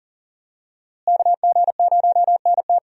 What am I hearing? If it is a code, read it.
Morse code, heard as a keyed tone, about 39 wpm, 700 Hz. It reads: KG0NT
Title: USA Call Signs - 5 Characters - Repeated 2x - 40wpm